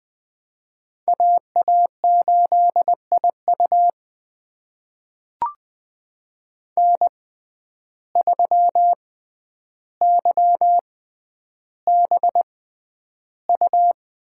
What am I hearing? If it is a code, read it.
Morse code, heard as a keyed tone, about 20 wpm, spaced as for 10 wpm, 700 Hz. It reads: AA8IU N3YBU
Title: USA Call Signs - 5 Characters - 6x Character Spacing - 20wpm